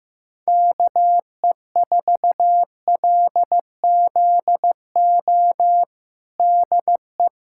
Morse code, heard as a keyed tone, about 15 wpm, 700 Hz. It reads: KE4LZO DE
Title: USA Callsign DE Callsign - 6 Characters - 15wpm